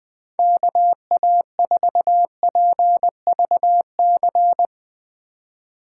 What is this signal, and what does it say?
Morse code, heard as a keyed tone, about 20 wpm, 700 Hz. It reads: KA4PVC
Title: USA Call Signs - 6 Characters - 4x Character Spacing - 20wpm